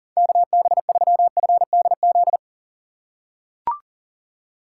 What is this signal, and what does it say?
Morse code, heard as a keyed tone, about 40 wpm, 700 Hz. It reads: KB3FDZ
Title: USA Call Signs - 6 Characters - 3x Character Spacing - 40wpm